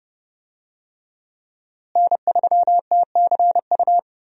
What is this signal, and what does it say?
Morse code, heard as a keyed tone, about 30 wpm, 700 Hz. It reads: N3TCU
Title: USA Call Signs - 5 Characters - Repeated 2x - 30wpm